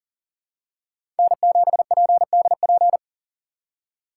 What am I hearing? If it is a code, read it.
Morse code, heard as a keyed tone, about 40 wpm, 700 Hz. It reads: N7PDP
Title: USA Call Signs - 5 Characters - 3x Character Spacing - 40wpm